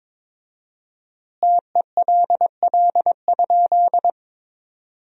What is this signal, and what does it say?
Morse code, heard as a keyed tone, about 22 wpm, 700 Hz. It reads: TELL?
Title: 4x Word-Spacing - Sentences from Top 200 Words - 22wpm